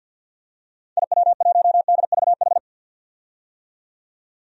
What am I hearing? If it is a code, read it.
Morse code, heard as a keyed tone, about 52 wpm, 700 Hz. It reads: IW1DVH
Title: DXCC-100 Call Signs - 6 Characters - 50wpm